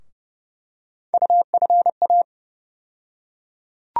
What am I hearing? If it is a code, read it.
Morse code, heard as a keyed tone, about 30 wpm, 700 Hz. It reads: UFA
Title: USA Call Sign Postfixes - 3 Letters 30wpm